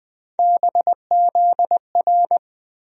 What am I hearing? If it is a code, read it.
Morse code, heard as a keyed tone, about 20 wpm, 700 Hz. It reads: BZR
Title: USA Call Sign Postfixes - 3 Letters 20wpm